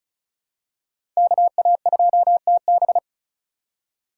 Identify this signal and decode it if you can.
Morse code, heard as a keyed tone, about 36 wpm, 700 Hz. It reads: KA2TB
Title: Code Groups - 5 Letter:Numbers - 35wpm